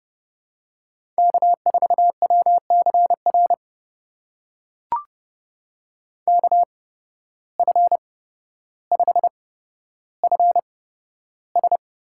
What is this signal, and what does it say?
Morse code, heard as a keyed tone, about 30 wpm, spaced as for 13 wpm, 700 Hz. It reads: K4WCR KF5FS
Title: USA Call Signs - 5 Characters - 8x Character Spacing - 30wpm